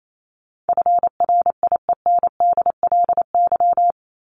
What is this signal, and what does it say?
Morse code, heard as a keyed tone, about 28 wpm, 700 Hz. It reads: FRIENDLY
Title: LY Suffix - Top 100 Words with Base Words 28wpm